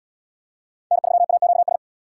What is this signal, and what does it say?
Morse code, heard as a keyed tone, about 67 wpm, 700 Hz. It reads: N3AQS